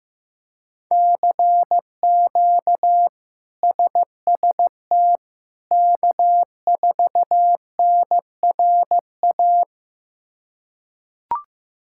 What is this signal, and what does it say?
Morse code, heard as a keyed tone, about 16 wpm, 700 Hz. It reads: CQ SST K4NRA
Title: K1USN SST QSO Practice - 15wpm